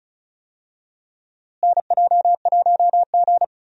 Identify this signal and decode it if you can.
Morse code, heard as a keyed tone, about 36 wpm, 700 Hz. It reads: NJ1G
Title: USA Call Signs - 4 Characters - Repeated 2x - 35wpm